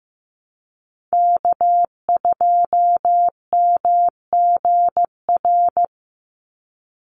Morse code, heard as a keyed tone, about 15 wpm, 700 Hz. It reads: K2MGR
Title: Call Signs Repeated 3x - 15wpm